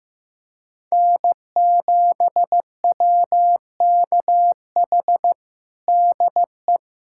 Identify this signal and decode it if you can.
Morse code, heard as a keyed tone, about 16 wpm, 700 Hz. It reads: N7WKH DE
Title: USA Callsign DE Callsign - 5 Characters - 15wpm